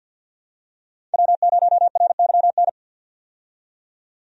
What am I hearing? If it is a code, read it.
Morse code, heard as a keyed tone, about 50 wpm, 700 Hz. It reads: W0RYN
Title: USA Call Signs - 5 Characters - 50wpm